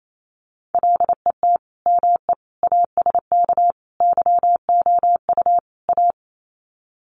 Morse code, heard as a keyed tone, about 28 wpm, 700 Hz. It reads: LET ME ASK YOU A
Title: Sets of 5 Words Encore 28wpm